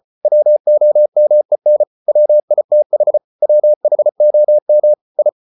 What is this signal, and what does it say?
Morse code, heard as a keyed tone, about 34 wpm, 600 Hz. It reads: WOMEN WITH WHOM I